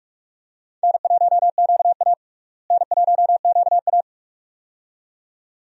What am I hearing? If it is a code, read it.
Morse code, heard as a keyed tone, about 46 wpm, 700 Hz. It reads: N1QA N1QA
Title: USA Call Signs - 4 Characters - Repeated 2x - 45wpm